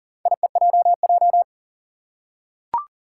Morse code, heard as a keyed tone, about 40 wpm, 700 Hz. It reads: IEJJ
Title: Four Challenging Letter-Number Combinations 40wpm